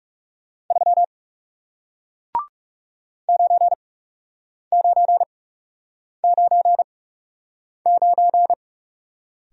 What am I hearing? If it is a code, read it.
Morse code, heard as a keyed tone, about 40 wpm, spaced as for 8 wpm, 700 Hz. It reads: 3 9999